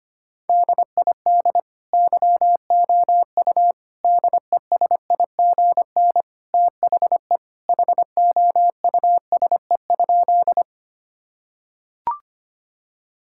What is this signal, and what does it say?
Morse code, heard as a keyed tone, about 25 wpm, 700 Hz. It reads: DID YOU DESIGN THE HOUSE?
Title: Sentences from Top 1000 Words - Review Part 5 - 25wpm